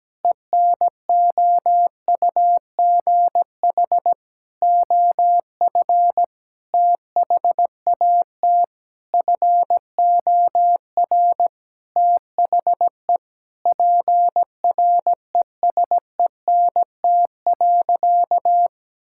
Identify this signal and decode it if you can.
Morse code, heard as a keyed tone, about 17 wpm, 700 Hz. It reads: ENOUGH OF THAT FOR THE PRESENT.
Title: Sentences from Top 1000 Words - Review Part 5 - 17wpm